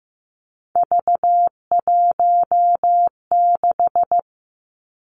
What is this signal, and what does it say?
Morse code, heard as a keyed tone, about 15 wpm, 700 Hz. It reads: V16
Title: Three Challenging Letter-Number Combinations 15wpm